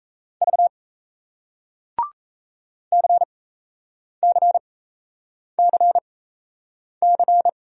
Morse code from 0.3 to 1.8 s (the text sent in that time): V